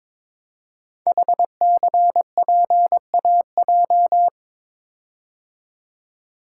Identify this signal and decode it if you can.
Morse code, heard as a keyed tone, about 23 wpm, 700 Hz. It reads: HCPAJ
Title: Code Groups - 5 Letters - 22wpm